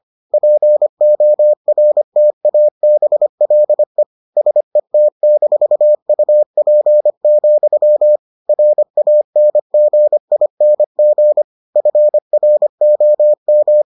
Morse code, heard as a keyed tone, about 25 wpm, 600 Hz. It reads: PORTABLE SET-UP, RANGING FROM